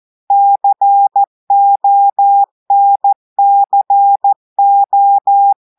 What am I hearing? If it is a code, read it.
Morse code, heard as a keyed tone, about 14 wpm, 800 Hz. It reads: CONCO
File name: audios / portugais / portugais texte en clair 1 vitesse 14 mots minutes